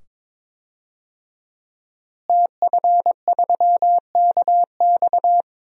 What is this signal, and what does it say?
Morse code, heard as a keyed tone, about 23 wpm, 700 Hz. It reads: TF3KX
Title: DXCC-100 Call Signs - 5 Characters - Repeated 2x - 22wpm